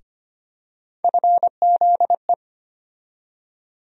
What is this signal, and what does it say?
Morse code, heard as a keyed tone, about 25 wpm, 700 Hz. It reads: FZE
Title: USA Call Sign Postfixes - 3 Letters 25wpm